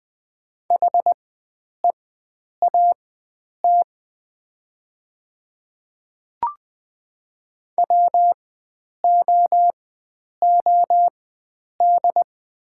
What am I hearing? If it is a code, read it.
Morse code, heard as a keyed tone, about 20 wpm, spaced as for 9 wpm, 700 Hz. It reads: HEAT WOOD
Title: 4x Character Spacing - Top 500 Words - 20wpm